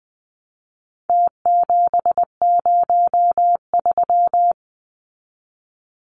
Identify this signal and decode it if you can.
Morse code, heard as a keyed tone, about 20 wpm, 700 Hz. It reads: T703
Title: Numbers - Contest Style - 4-Digit 20wpm